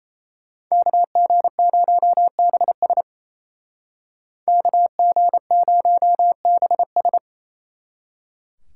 Morse code, heard as a keyed tone, about 34 wpm, 700 Hz. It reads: KG0BS KG0BS
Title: Speed Racing - USA 2x2 Call Signs 28wpm